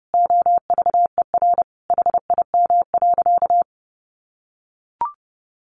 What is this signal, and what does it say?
Morse code, heard as a keyed tone, about 30 wpm, 700 Hz. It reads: OVER HIM.
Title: Additional Easy Phrases 30wpm